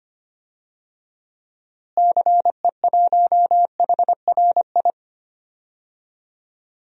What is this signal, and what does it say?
Morse code, heard as a keyed tone, about 25 wpm, 700 Hz. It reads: CE1HRI
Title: DXCC-100 Call Signs - 6 Characters - Repeated 2x - 25wpm